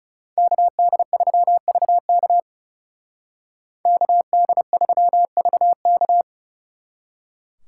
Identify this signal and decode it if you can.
Morse code, heard as a keyed tone, about 36 wpm, 700 Hz. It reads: KD3VK KD3VK
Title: Speed Racing - USA 2x2 Call Signs 30wpm